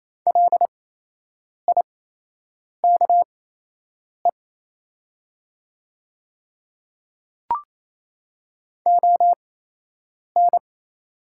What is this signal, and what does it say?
Morse code, heard as a keyed tone, about 28 wpm, spaced as for 8 wpm, 700 Hz. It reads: LIKE ON
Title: 8x Character Spacing - Top 100 Words - 28wpm